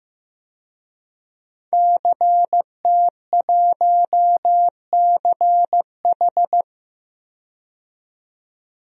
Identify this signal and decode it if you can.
Morse code, heard as a keyed tone, about 15 wpm, 700 Hz. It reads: CT1CH
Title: DXCC-100 Call Signs - 5 Characters - 15wpm